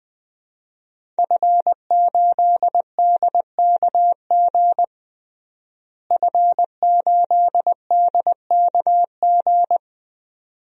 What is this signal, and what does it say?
Morse code, heard as a keyed tone, about 20 wpm, 700 Hz. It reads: F8DKG F8DKG
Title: DXCC-100 Call Signs - 5 Characters - Repeated 2x - 20wpm